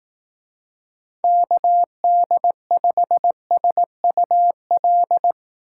Morse code, heard as a keyed tone, about 18 wpm, 700 Hz. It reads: KD5SUL
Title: Speed Racing - USA 2x3 Call Signs 15wpm